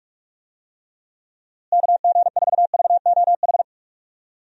Morse code, heard as a keyed tone, about 44 wpm, 700 Hz. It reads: KG4VOH
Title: USA Call Signs - 6 Characters - Repeated 3x - 45wpm